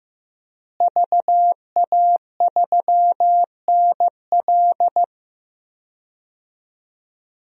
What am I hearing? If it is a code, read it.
Morse code, heard as a keyed tone, about 16 wpm, 700 Hz. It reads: VA3NL
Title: DXCC-100 Call Signs - 5 Characters - Repeated 3x - 15wpm